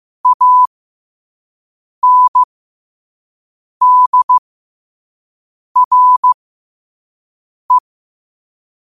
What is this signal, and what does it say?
Morse code, heard as a keyed tone, about 15 wpm, spaced as for 5 wpm, 1000 Hz. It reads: ANDRE
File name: qotdfiles / 5wpm